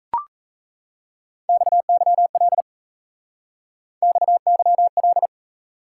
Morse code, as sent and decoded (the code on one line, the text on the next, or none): -..- -.-- .-.. / -..- -.-- .-..
XYL XYL